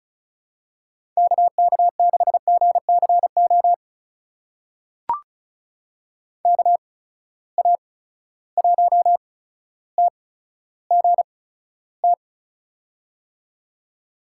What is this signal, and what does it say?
Morse code, heard as a keyed tone, about 36 wpm, spaced as for 15 wpm, 700 Hz. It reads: KK6GCO KA1TGT
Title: USA Call Signs - 8x Character Spacing - 35wpm